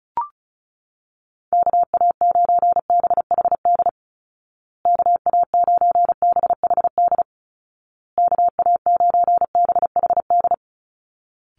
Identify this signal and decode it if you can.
Morse code, heard as a keyed tone, about 34 wpm, 700 Hz. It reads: KA9BHD KA9BHD KA9BHD
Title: Call Signs Repeated 2x - 35wpm